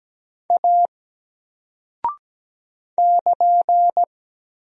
Morse code, heard as a keyed tone, about 17 wpm, 700 Hz. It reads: A <KN>
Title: Single Character (Letter:Number:Punctuation:Prosign) 17wpm